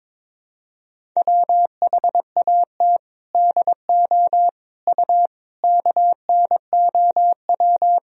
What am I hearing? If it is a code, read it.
Morse code, heard as a keyed tone, about 22 wpm, 700 Hz. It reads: WHAT DO U KNOW
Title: Sentences from Top 100 Words and Abbreviations 22wpm